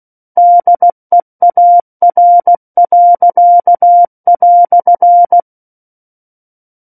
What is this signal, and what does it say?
Morse code, heard as a keyed tone, about 16 wpm, 700 Hz. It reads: DEAR."
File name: Book One / 07-Chapter 7 - 16 WPM